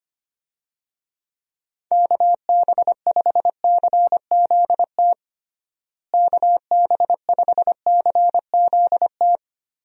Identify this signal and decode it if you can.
Morse code, heard as a keyed tone, about 25 wpm, 700 Hz. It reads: KB5CZT KB5CZT
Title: USA Call Signs - 6 Characters - Repeated 3x - 25wpm